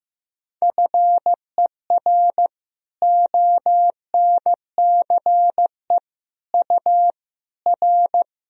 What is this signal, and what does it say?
Morse code, heard as a keyed tone, about 15 wpm, 700 Hz. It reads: FER ONCE U R
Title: Sentences from Top 300 Words and Abbreviations 15wpm